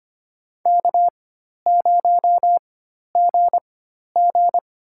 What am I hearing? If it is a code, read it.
Morse code, heard as a keyed tone, about 25 wpm, spaced as for 12 wpm, 700 Hz. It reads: K0GG
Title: USA Call Signs - 4 Characters - 4x Character Spacing - 25wpm